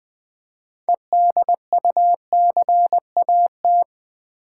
Morse code, heard as a keyed tone, about 20 wpm, 700 Hz. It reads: EDUCAT
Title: Build A Word - 3 Letters At A Time - 9-Letter Words - 20wpm